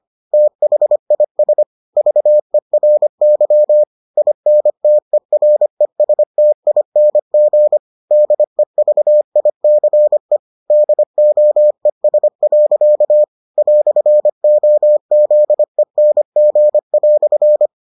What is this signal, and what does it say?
Morse code, heard as a keyed tone, about 25 wpm, 600 Hz. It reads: THIS VERY INTERESTING DEVICE DOES. "OZENG"